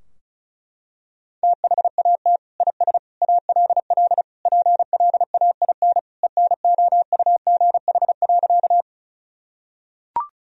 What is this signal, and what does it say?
Morse code, heard as a keyed tone, about 34 wpm, 700 Hz. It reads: THAT IS ALL PLAIN ENOUGH.
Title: Sentences from Top 1000 Words - Review Part 2 - 35wpm